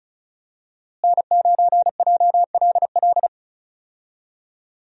Morse code, heard as a keyed tone, about 36 wpm, 700 Hz. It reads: N9JLL
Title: USA Call Signs - 5 Characters - Repeated 3x - 35wpm